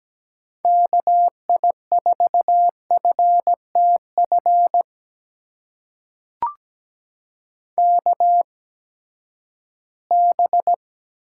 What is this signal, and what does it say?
Morse code, heard as a keyed tone, about 17 wpm, spaced as for 12 wpm, 700 Hz. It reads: KI4FTF KB
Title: USA Call Signs - 6 Characters - 8x Character Spacing - 17wpm